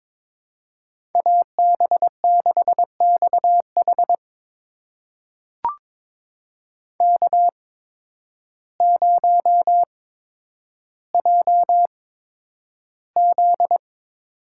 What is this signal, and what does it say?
Morse code, heard as a keyed tone, about 21 wpm, spaced as for 10 wpm, 700 Hz. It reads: AB6XH K0JZ
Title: USA Call Signs - 8x Character Spacing - 22wpm